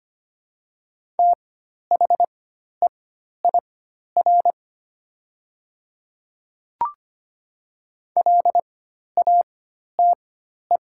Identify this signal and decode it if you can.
Morse code, heard as a keyed tone, about 25 wpm, spaced as for 12 wpm, 700 Hz. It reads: THEIR LATE
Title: 4x Character Spacing - Top 300 Words - 25wpm